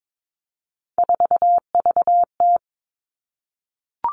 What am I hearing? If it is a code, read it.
Morse code, heard as a keyed tone, about 22 wpm, 700 Hz. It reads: 4VT
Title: All Cut Numbers - 3-Digit 22wpm